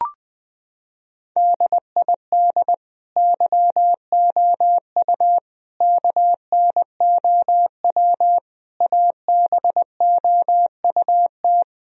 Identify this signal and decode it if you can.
Morse code, heard as a keyed tone, about 21 wpm, 700 Hz. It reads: DID YOU KNOW ABOUT
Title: Sentences from Top 1000 Words - Review Part 4 - 20wpm